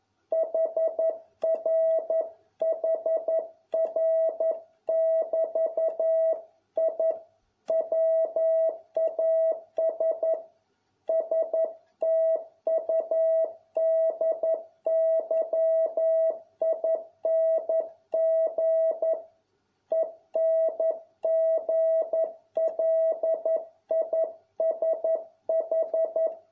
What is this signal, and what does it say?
Morse code, heard as a keyed tone, about 11 wpm, 650 Hz. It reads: HRHR<BT>I WAS STUDYING ENGLISH